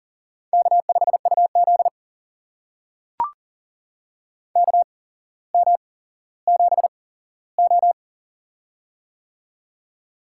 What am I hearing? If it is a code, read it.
Morse code, heard as a keyed tone, about 40 wpm, spaced as for 17 wpm, 700 Hz. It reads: K5UZ KM7O
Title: USA Call Signs - 4 Characters - 8x Character Spacing - 40wpm